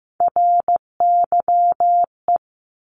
Morse code, heard as a keyed tone, about 15 wpm, 700 Hz. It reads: RYE
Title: Multi-letter Phonemes 15wpm